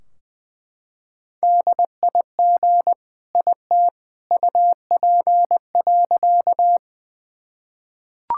Morse code, heard as a keyed tone, about 20 wpm, 700 Hz. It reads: DIG IT UP.
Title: Sentences from Top 1600 Words 20wpm